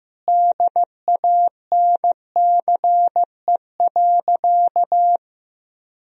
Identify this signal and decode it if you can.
Morse code, heard as a keyed tone, about 15 wpm, 700 Hz. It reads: DANCE.
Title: Sentences from Top 1000 Words - Review Part 4 - 15wpm